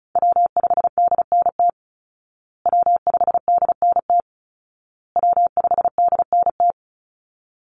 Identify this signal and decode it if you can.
Morse code, heard as a keyed tone, about 34 wpm, 700 Hz. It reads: W5DNT W5DNT W5DNT
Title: Call Signs Repeated 2x - 35wpm